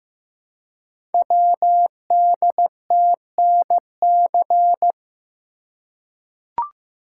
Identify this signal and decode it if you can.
Morse code, heard as a keyed tone, about 15 wpm, 700 Hz. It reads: WDTNC
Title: Code Groups - 5 Letters - 15wpm